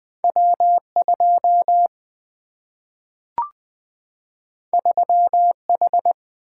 Morse code, heard as a keyed tone, about 20 wpm, 700 Hz. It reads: W 2 3 H
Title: Single-Letter Single-Number 20wpm